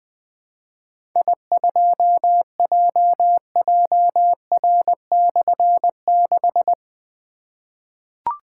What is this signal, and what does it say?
Morse code, heard as a keyed tone, about 20 wpm, 700 Hz. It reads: I2JJR/6
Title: Difficult Call Signs 20wpm